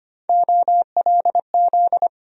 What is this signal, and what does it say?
Morse code, heard as a keyed tone, about 25 wpm, 700 Hz. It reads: OLZ
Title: USA Call Sign Postfixes - 3 Letters 25wpm